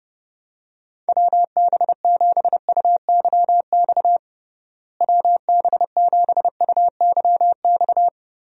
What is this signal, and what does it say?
Morse code, heard as a keyed tone, about 30 wpm, 700 Hz. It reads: WB7UYX WB7UYX
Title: USA Call Signs - 6 Characters - Repeated 3x - 30wpm